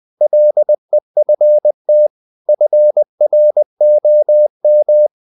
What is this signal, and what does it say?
Morse code, heard as a keyed tone, about 20 wpm, 600 Hz. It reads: LEFT FROM
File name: the old man and the sea / the old man and the sea part 1 20wpm